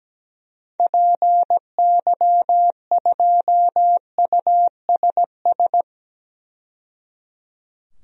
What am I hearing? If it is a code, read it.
Morse code, heard as a keyed tone, about 17 wpm, 700 Hz. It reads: PY2USS
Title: DXCC-100 Call Signs - 6 Characters - 17wpm